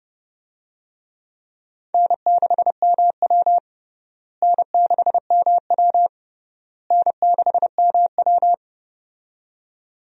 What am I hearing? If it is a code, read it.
Morse code, heard as a keyed tone, about 30 wpm, 700 Hz. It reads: N6MW N6MW N6MW
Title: USA Call Signs - 4 Characters - Repeated 3x - 30wpm